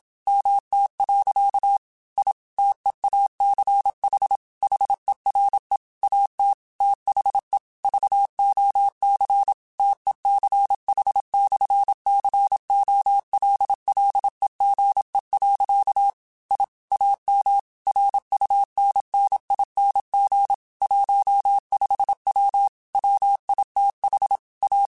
Morse code, heard as a keyed tone, about 27 wpm, 800 Hz. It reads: MT. I TEACH HERE AT THE VOC TECH/COLLEGE. I AM RUNNING 15W WITH A
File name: QSOs / QSOs-mp3s / WD0BGP 1@25